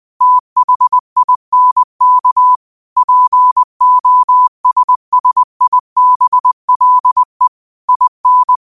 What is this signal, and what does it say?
Morse code, heard as a keyed tone, about 20 wpm, 1000 Hz. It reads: THINK POSSIBLE IN
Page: New quotes every day in morse code at 20 Words per minute.